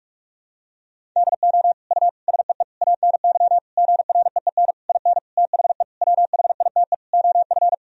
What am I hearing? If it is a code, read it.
Morse code, heard as a keyed tone, about 46 wpm, 700 Hz. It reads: DO U SEE ANY GREEN IN THE WHITE OF
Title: Sentences from Top 500 Words and Abbreviations 45wpm